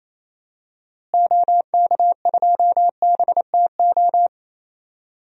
Morse code, heard as a keyed tone, about 28 wpm, 700 Hz. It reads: OK2BTO
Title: DXCC-100 Call Signs - 6 Characters - Repeated 2x - 28wpm